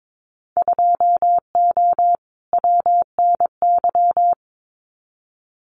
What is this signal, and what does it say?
Morse code, heard as a keyed tone, about 22 wpm, 700 Hz. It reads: 2O WNY
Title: Winter Field Day Exchanges Encore 22wpm